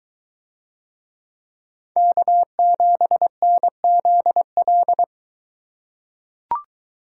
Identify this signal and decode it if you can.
Morse code, heard as a keyed tone, about 23 wpm, 700 Hz. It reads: K7NZL